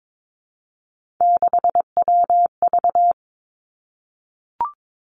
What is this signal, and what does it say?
Morse code, heard as a keyed tone, about 22 wpm, 700 Hz. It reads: 6WV
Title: Three Challenging Letter-Number Combinations 22wpm